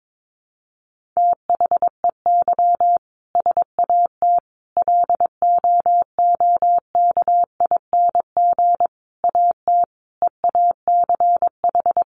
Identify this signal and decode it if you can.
Morse code, heard as a keyed tone, about 22 wpm, 700 Hz. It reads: THEY SAT LOOKING AT EACH
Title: A Sound Of Thunder 22wpm